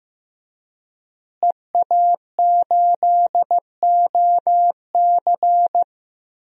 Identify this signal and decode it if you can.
Morse code, heard as a keyed tone, about 15 wpm, 700 Hz. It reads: EA8OC
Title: DXCC-100 Call Signs - 5 Characters - Repeated 3x - 15wpm